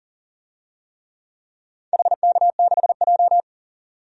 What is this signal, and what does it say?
Morse code, heard as a keyed tone, about 40 wpm, 700 Hz. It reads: HK6J
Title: DXCC-100 Call Signs - 4 Characters - 40wpm